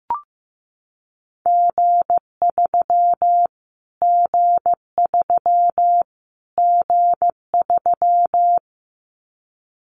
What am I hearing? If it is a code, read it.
Morse code, heard as a keyed tone, about 15 wpm, 700 Hz. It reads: G3 G3 G3
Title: Callsign Prefixes for Top 25 DXCC - Repeated 3x - 15wpm